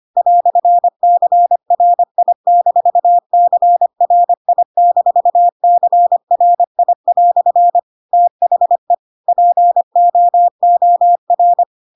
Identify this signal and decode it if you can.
Morse code, heard as a keyed tone, about 25 wpm, 700 Hz. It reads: "CRI-CRI-CRI" THE POOR
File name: Pinocchio / Pinocchio ch4 25 25 ik1hge